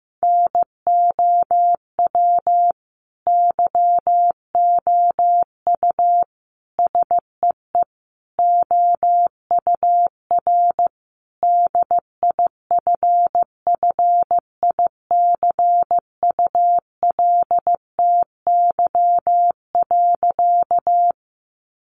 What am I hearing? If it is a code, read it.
Morse code, heard as a keyed tone, about 15 wpm, 700 Hz. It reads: NOW YOU SEE OUR DIFFICULTY.
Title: Sentences from Top 1500 Words 15wpm